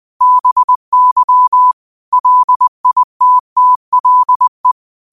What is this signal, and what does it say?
Morse code, heard as a keyed tone, about 20 wpm, 1000 Hz. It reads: BY LITTLE
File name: qotdfiles / 20wpm